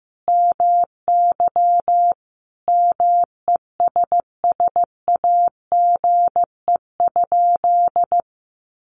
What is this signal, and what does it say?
Morse code, heard as a keyed tone, about 15 wpm, 700 Hz. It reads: MY MESSAGE?
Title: Sentences from Top 1100 Words 15wpm